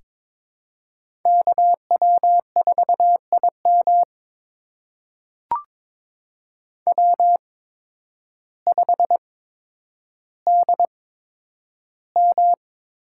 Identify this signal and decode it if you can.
Morse code, heard as a keyed tone, about 22 wpm, spaced as for 10 wpm, 700 Hz. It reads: KW4IM W5DM
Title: USA Call Signs - 5 Characters - 8x Character Spacing - 22wpm